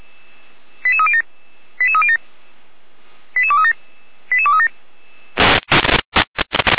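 Five-tone selective call (ZVEI), heard as quick runs of five tones, tones of about 70 ms. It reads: 88307 88307 88237 88237
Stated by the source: ZVEI five-tone selective call received on an SDR, from the Feuerwehr Orlamünde audio server.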